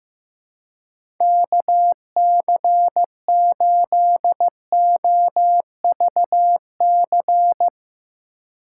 Morse code, heard as a keyed tone, about 15 wpm, 700 Hz. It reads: KC8OVC
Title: USA Call Signs - 6 Characters - 2x Character Spacing - 15wpm